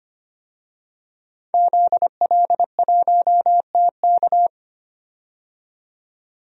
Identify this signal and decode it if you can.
Morse code, heard as a keyed tone, about 25 wpm, 700 Hz. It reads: ZL1TK
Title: DXCC-100 Call Signs - 5 Characters - 25wpm